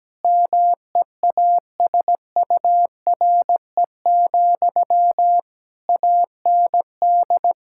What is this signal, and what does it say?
Morse code, heard as a keyed tone, about 17 wpm, 700 Hz. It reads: MEASURE, AND
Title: Sentences from Top 1000 Words - Review Part 5 - 17wpm